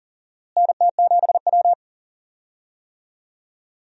Morse code, heard as a keyed tone, about 40 wpm, 700 Hz. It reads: NT7W
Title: USA Call Signs - 4 Characters - Repeated 2x - 40wpm